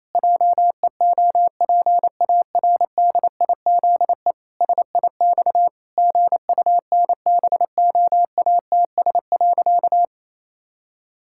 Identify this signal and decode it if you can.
Morse code, heard as a keyed tone, about 27 wpm, 700 Hz. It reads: JEOPARDIZE SIX GUNBOATS.
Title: Pangram Challenge v2 - 28wpm